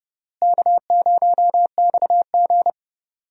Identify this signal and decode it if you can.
Morse code, heard as a keyed tone, about 30 wpm, 700 Hz. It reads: K0XG
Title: USA Call Signs - 4 Characters - Repeated 3x - 30wpm